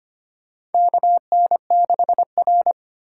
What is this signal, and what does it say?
Morse code, heard as a keyed tone, about 25 wpm, 700 Hz. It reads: KN6R
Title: USA Call Signs - 4 Characters - 6x Character Spacing - 25wpm